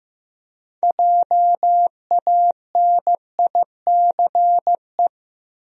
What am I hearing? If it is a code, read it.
Morse code, heard as a keyed tone, about 15 wpm, 700 Hz. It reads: JANICE
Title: USA Common Names - 15wpm